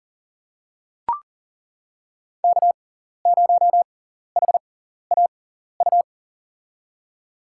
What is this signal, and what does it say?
Morse code, heard as a keyed tone, about 38 wpm, spaced as for 14 wpm, 700 Hz. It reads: K0HAU
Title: USA Call Signs - 5 Characters - 6x Character Spacing - 40wpm